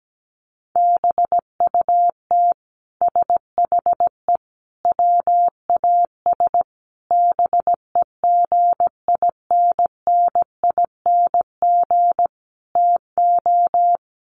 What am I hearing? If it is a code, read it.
Morse code, heard as a keyed tone, about 17 wpm, 700 Hz. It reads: BUT SHE WAS BEGINNING TO
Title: Sentences from Top 1500 Words 17wpm